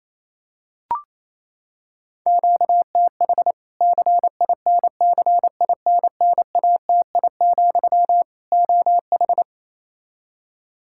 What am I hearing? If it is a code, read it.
Morse code, heard as a keyed tone, about 28 wpm, 700 Hz. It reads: QTH CINCINNATI, OH
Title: Practice QSO - Exchange 1 - Beginning - 28wpm